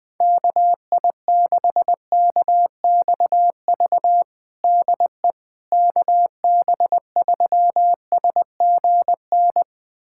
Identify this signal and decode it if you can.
Morse code, heard as a keyed tone, about 21 wpm, 700 Hz. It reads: KI6KXV DE KB3SGN
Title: USA Callsign DE Callsign 20wpm